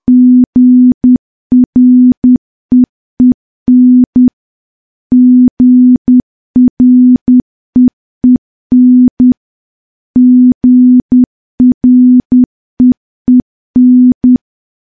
Morse code, heard as a keyed tone, about 10 wpm, 250 Hz. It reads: GREEN GREEN GREEN
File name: morse / morse-training-10WPM